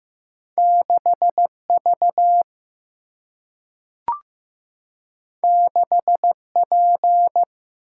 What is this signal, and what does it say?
Morse code, heard as a keyed tone, about 15 wpm, 700 Hz. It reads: 6 V 6 P
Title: Single-Letter Single-Number 15wpm